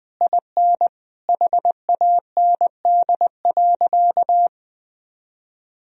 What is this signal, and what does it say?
Morse code, heard as a keyed tone, about 20 wpm, 700 Hz. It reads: IN HAND.
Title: Sentences from Top 1000 Words - Review Part 4 - 20wpm